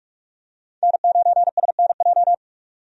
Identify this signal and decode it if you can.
Morse code, heard as a keyed tone, about 43 wpm, 700 Hz. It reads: N9SNJ
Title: USA Call Signs - 5 Characters - 45wpm